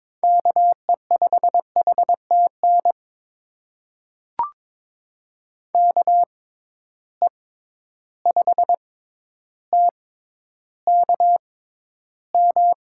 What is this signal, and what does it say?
Morse code, heard as a keyed tone, about 22 wpm, spaced as for 11 wpm, 700 Hz. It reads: KE5HTN KE5TKM
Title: USA Call Signs - 6 Characters - 6x Character Spacing - 22wpm